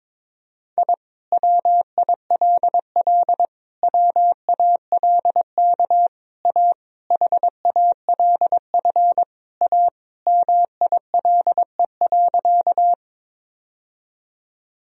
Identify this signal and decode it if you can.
Morse code, heard as a keyed tone, about 22 wpm, 700 Hz. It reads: I WILL WALK A HALF A MILE.
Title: Sentences from Top 1000 Words - Review Part 1 - 22wpm